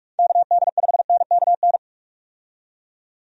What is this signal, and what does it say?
Morse code, heard as a keyed tone, about 46 wpm, 700 Hz. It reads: KD5NKN
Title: USA Call Signs - 6 Characters - 45wpm